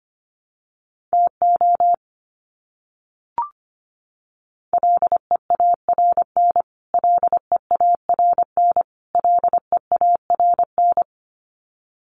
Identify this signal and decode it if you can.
Morse code, heard as a keyed tone, about 25 wpm, 700 Hz. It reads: TO LEARN LEARN LEARN
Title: Top 500 Words Repeated 25wpm